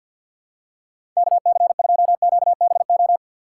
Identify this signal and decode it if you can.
Morse code, heard as a keyed tone, about 53 wpm, 700 Hz. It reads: KC2QBO
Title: USA Call Signs - 6 Characters - 6x Character Spacing - 50wpm